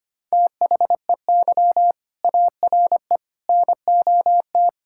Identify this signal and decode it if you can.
Morse code, heard as a keyed tone, about 25 wpm, 700 Hz. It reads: THEY ARE NOT
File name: practice / 25wpm / Sentences with 4-Word Phrases - 25wpm